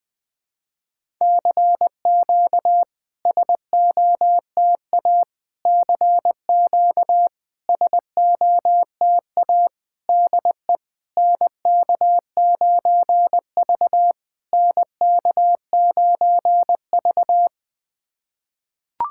Morse code, heard as a keyed tone, about 20 wpm, 700 Hz. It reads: CQ SOTA CQ SOTA DE NK9V NK9V
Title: SOTA QSO Practice 20wpm